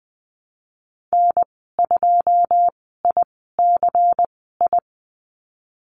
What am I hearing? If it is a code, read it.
Morse code, heard as a keyed tone, about 20 wpm, spaced as for 14 wpm, 700 Hz. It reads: N2ICI
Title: 122 Lesson 20g - Number 3 - Call Signs 20wpm